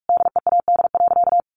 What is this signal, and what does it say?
Morse code, heard as a keyed tone, about 44 wpm, 700 Hz. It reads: DEAD.
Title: Sentences from Top 1500 Words 45wpm